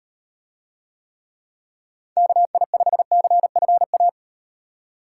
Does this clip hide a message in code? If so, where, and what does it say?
Yes: Morse code from 2.2 to 4.1 s, heard as a keyed tone, about 36 wpm, 700 Hz. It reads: KI5CFA